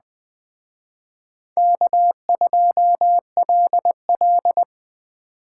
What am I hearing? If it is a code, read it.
Morse code, heard as a keyed tone, about 21 wpm, 700 Hz. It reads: K2LL
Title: Speed Racing - USA 1x2 Call Signs 17wpm